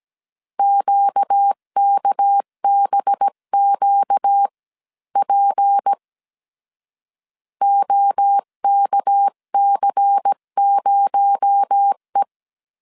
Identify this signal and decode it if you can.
Morse code, heard as a keyed tone, about 17 wpm, spaced as for 14 wpm, 800 Hz. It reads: QKBQP OKC0E
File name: CW / DET / 1 6 1500C S C 16wpm 2733